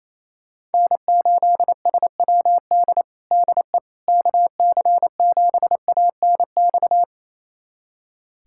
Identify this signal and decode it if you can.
Morse code, heard as a keyed tone, about 28 wpm, 700 Hz. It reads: N8SWD DE KC7ANX
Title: USA Callsign DE Callsign 28wpm